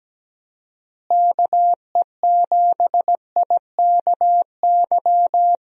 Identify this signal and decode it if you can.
Morse code, heard as a keyed tone, about 17 wpm, 700 Hz. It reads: KE7IKY
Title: USA Call Signs - 6 Characters - Repeated 2x - 17wpm